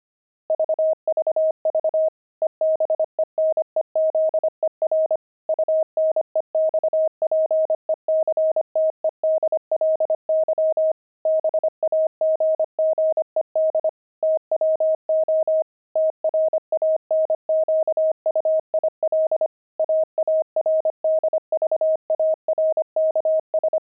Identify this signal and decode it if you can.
Morse code, heard as a keyed tone, about 25 wpm, 630 Hz. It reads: VVV EBENEZER UNEXPECTEDLY BAGGED TWO TRANQUIL AARDVARKS
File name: pangram-fs-25-cs-25-aardvark